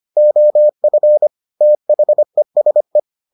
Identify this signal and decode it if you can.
Morse code, heard as a keyed tone, about 25 wpm, 600 Hz. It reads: OF THESE